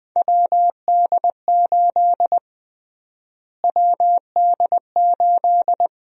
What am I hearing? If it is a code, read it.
Morse code, heard as a keyed tone, about 21 wpm, 700 Hz. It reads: WD8 WD8
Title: USA Call Sign Prefixes - Repeated 2x - 20wpm